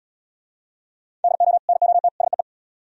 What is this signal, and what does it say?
Morse code, heard as a keyed tone, about 70 wpm, 700 Hz. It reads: DO NOT SEE